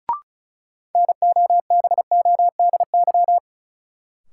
Morse code, heard as a keyed tone, about 36 wpm, 700 Hz. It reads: NOBODY
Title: Sentences from Top 1600 Words - New Word Drill - 35wpm